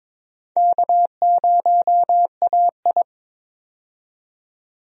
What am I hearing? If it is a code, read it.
Morse code, heard as a keyed tone, about 21 wpm, 700 Hz. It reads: K0AI
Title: USA Call Signs - 4 Characters - 22wpm